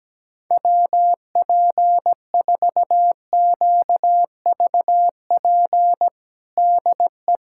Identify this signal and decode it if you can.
Morse code, heard as a keyed tone, about 17 wpm, 700 Hz. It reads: WP4QVP DE
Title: USA Callsign DE Callsign 17wpm